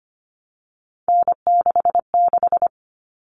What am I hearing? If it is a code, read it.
Morse code, heard as a keyed tone, about 25 wpm, 700 Hz. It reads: N66
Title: All Cut Numbers - 3-Digit 25wpm